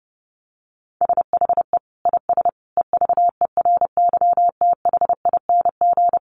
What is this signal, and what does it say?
Morse code, heard as a keyed tone, about 30 wpm, 700 Hz. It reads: SHE IS EVERYTHING
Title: Sentences from Top 700 Words 30wpm